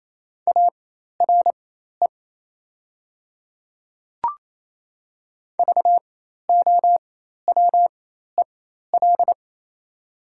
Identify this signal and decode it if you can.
Morse code, heard as a keyed tone, about 28 wpm, spaced as for 13 wpm, 700 Hz. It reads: ARE VOWEL
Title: 4x Character Spacing - Top 500 Words - 28wpm